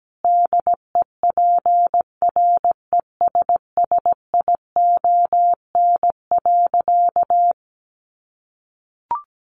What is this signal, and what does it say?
Morse code, heard as a keyed tone, about 17 wpm, 700 Hz. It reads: DEPRESSION.
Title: Sentences from Top 1300 Words 17wpm